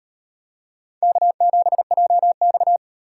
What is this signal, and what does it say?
Morse code, heard as a keyed tone, about 35 wpm, 700 Hz. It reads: K7JX